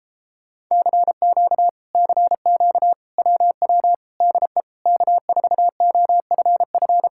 Morse code, heard as a keyed tone, about 33 wpm, 700 Hz. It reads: CQ CQ WW DE K4OFF